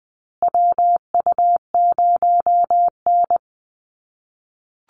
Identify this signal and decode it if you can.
Morse code, heard as a keyed tone, about 20 wpm, 700 Hz. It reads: WU0N
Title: USA Call Signs Repeated 2x 20wpm